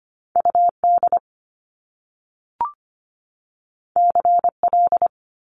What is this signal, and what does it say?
Morse code, heard as a keyed tone, about 25 wpm, 700 Hz. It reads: UD CL
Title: Top Two Letter Combinations in Words 25wpm